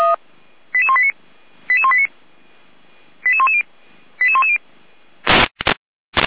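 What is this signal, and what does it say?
Five-tone selective call (ZVEI), heard as quick runs of five tones, tones of about 70 ms. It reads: 88189 88189 88110 88110
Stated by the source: ZVEI five-tone selective call 88352 received by SDR.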